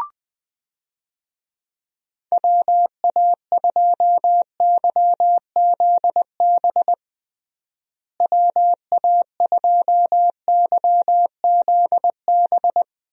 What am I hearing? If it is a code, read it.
Morse code, heard as a keyed tone, about 20 wpm, 700 Hz. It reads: WA2YZB WA2YZB
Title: USA Call Signs - 6 Characters - Repeated 3x - 20wpm